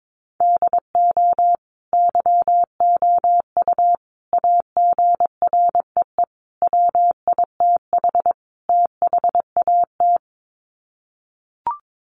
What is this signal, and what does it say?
Morse code, heard as a keyed tone, about 22 wpm, 700 Hz. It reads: DO YOU AGREE WITH THAT
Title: Sets of 5 Words Encore 22wpm